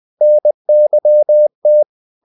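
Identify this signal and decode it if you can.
Morse code, heard as a keyed tone, about 20 wpm, 600 Hz. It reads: NYT
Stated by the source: Morse code practice of short Danish words at 20 wpm, from CW med Gnister.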